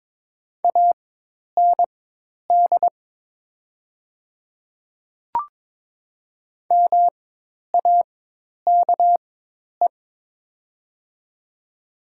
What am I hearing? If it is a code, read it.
Morse code, heard as a keyed tone, about 21 wpm, spaced as for 10 wpm, 700 Hz. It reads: AND MAKE
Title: 4x Character Spacing - Top 100 Words - 22wpm